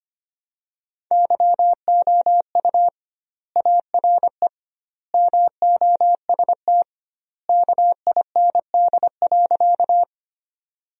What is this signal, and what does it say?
Morse code, heard as a keyed tone, about 25 wpm, 700 Hz. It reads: YOU ARE MOST KIND.
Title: 2x Word-Spacing - Sentences from Top 200 Words - 25wpm